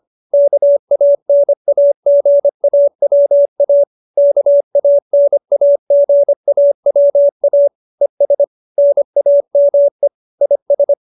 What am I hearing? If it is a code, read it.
Morse code, heard as a keyed tone, about 25 wpm, 570 Hz. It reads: KANAGAWA KANAGAWA ES NAME IS